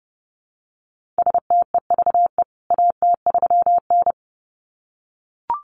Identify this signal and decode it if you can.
Morse code, heard as a keyed tone, about 30 wpm, 700 Hz. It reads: STEVE AT3N
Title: CWT Exchange 30wpm